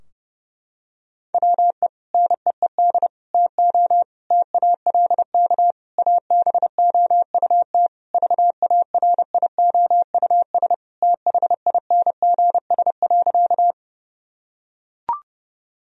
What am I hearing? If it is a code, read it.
Morse code, heard as a keyed tone, about 30 wpm, 700 Hz. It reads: WE NEED TO TALK ABOUT VARIOUS THINGS.
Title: Sentences from Top 1000 Words - Review Part 4 - 30wpm